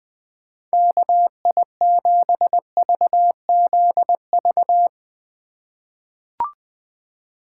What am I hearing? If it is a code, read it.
Morse code, heard as a keyed tone, about 21 wpm, 700 Hz. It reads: KI7VZV
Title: USA Call Signs - 6 Characters - 4x Character Spacing - 20wpm